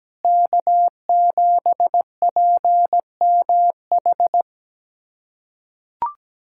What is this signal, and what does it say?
Morse code, heard as a keyed tone, about 17 wpm, 700 Hz. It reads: K7PMH
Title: USA Callsign DE Callsign - 5 Characters - 17wpm